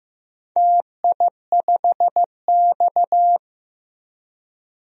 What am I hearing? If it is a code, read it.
Morse code, heard as a keyed tone, about 15 wpm, 700 Hz. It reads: TI5X
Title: DXCC-100 Call Signs - 4 Characters - Repeated 3x - 15wpm